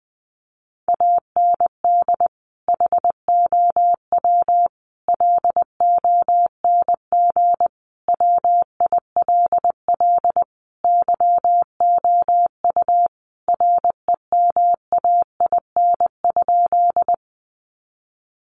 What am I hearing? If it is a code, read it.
Morse code, heard as a keyed tone, about 20 wpm, 700 Hz. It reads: AND HOW LONG WILL YOU REMAIN?
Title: Sentences from Top 1200 Words 20wpm